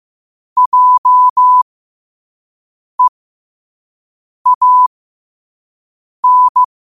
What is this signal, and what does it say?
Morse code, heard as a keyed tone, about 15 wpm, spaced as for 5 wpm, 1000 Hz. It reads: JEAN
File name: qotdfiles / 5wpm